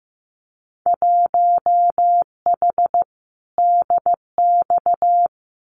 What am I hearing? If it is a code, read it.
Morse code, heard as a keyed tone, about 15 wpm, 700 Hz. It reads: 1H DX
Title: Winter Field Day Exchanges 15wpm